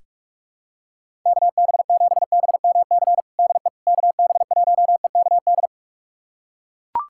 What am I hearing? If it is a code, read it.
Morse code, heard as a keyed tone, about 41 wpm, 700 Hz. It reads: KB7BMC DE KB1EKD
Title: USA Callsign DE Callsign - 6 Characters - 45wpm